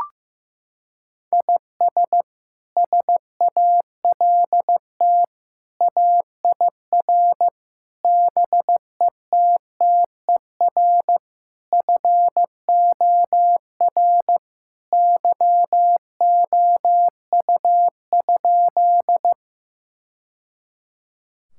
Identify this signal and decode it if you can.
Morse code, heard as a keyed tone, about 15 wpm, 700 Hz. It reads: IS SALT AIR BETTER FOR YOU?
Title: Sentences from Top 1000 Words - Review Part 5 - 15wpm